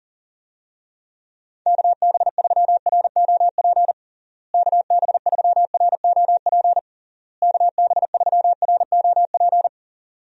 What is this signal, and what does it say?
Morse code, heard as a keyed tone, about 40 wpm, 700 Hz. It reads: KB3ROP KB3ROP KB3ROP
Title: USA Call Signs - 6 Characters - Repeated 3x - 40wpm